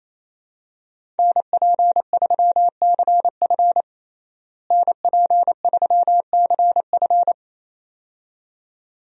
Morse code, heard as a keyed tone, about 28 wpm, 700 Hz. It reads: NP3CF NP3CF
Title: USA Call Signs - 5 Characters - Repeated 2x - 28wpm